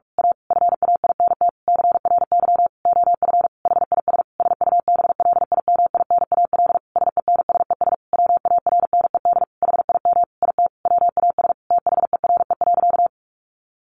Morse code, heard as a keyed tone, about 44 wpm, 700 Hz. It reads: A FAINT CRY OF HIS SUBLIMINAL SENSES WARNED HIM IT WAS THERE.
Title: A Sound Of Thunder 45wpm